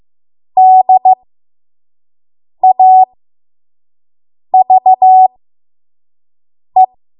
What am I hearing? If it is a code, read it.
Morse code, heard as a keyed tone, about 15 wpm, spaced as for 5 wpm, 750 Hz. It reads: DAVE
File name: Morse / 05WPM